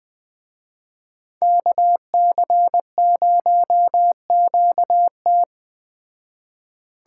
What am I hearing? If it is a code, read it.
Morse code, heard as a keyed tone, about 20 wpm, 700 Hz. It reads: KC0QT
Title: Speed Racing - USA Call Signs 17wpm